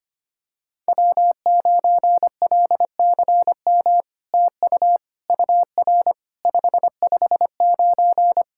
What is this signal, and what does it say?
Morse code, heard as a keyed tone, about 25 wpm, 700 Hz. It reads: W9LCM TU UR 559
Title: SOTA QSO Practice 25wpm